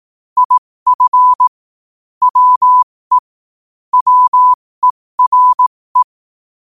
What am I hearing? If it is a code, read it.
Morse code, heard as a keyed tone, about 18 wpm, spaced as for 15 wpm, 1000 Hz. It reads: IF WE WERE
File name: qotdfiles / 15wpm